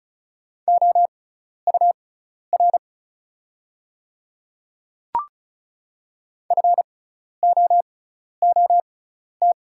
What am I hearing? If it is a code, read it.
Morse code, heard as a keyed tone, about 36 wpm, spaced as for 12 wpm, 700 Hz. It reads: OUR FOOT
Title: 6x Character Spacing - Top 500 Words - 35wpm